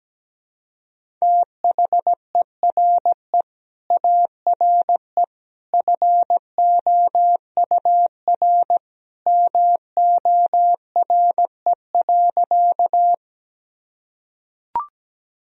Morse code, heard as a keyed tone, about 17 wpm, 700 Hz. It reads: THERE ARE FOUR MORE.
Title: Sentences from Top 1000 Words - Review Part 1 - 17wpm